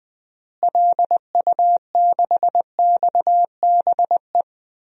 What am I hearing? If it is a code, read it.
Morse code, heard as a keyed tone, about 20 wpm, 700 Hz. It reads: LU6XBE
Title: DXCC-100 Call Signs - 6 Characters - Repeated 2x - 20wpm